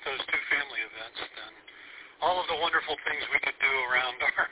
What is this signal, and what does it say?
Emergency Alert System (SAME) tones, heard as harsh buzzing bursts, with speech. It is heard from 0.2 to 3.5 s.